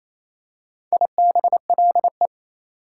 Morse code, heard as a keyed tone, about 29 wpm, 690 Hz. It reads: IBLE
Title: Prefixes and Suffixes 28wpm